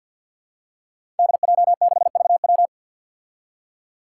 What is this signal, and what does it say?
Morse code, heard as a keyed tone, about 53 wpm, 700 Hz. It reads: DJ6VW
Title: Five Challenging Letter-Number Combinations 50wpm